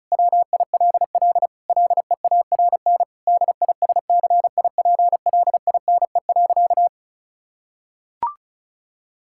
Morse code, heard as a keyed tone, about 36 wpm, 700 Hz. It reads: WILL LEARN DISCIPLINE.
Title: Sentences from Top 1000 Words - Review Part 6 - 35wpm